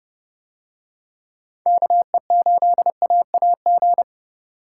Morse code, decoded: KE8AAG

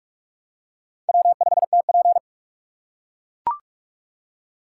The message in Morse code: W5TP

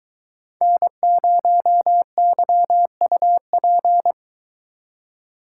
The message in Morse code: N0YUP

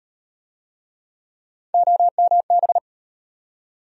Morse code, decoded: OMB

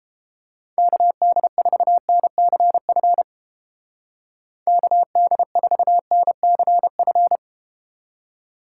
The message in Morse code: KD4NCF KD4NCF

